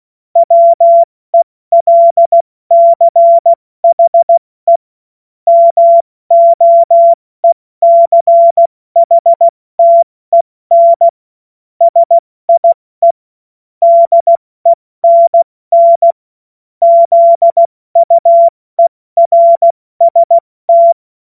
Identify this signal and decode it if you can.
Morse code, heard as a keyed tone, about 16 wpm, 680 Hz. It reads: WELCHE MOECHTEN SIE DENN ZUERST
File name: Morsekurs / Witze / Witz 13. Juni 2023 70 F80